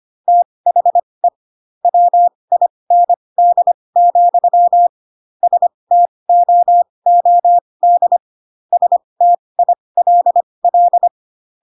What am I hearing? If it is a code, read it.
Morse code, heard as a keyed tone, about 25 wpm, spaced as for 20 wpm, 700 Hz. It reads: THE WIND, STOOD STILL